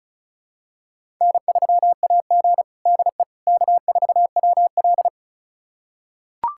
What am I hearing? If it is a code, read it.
Morse code, heard as a keyed tone, about 34 wpm, 700 Hz. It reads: N3AG DE K4WL
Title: USA Callsign DE Callsign - 4 Characters - 35wpm